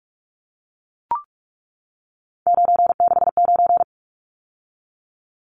Morse code, heard as a keyed tone, about 44 wpm, 700 Hz. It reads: 969
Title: Numbers - 3-Digit 45wpm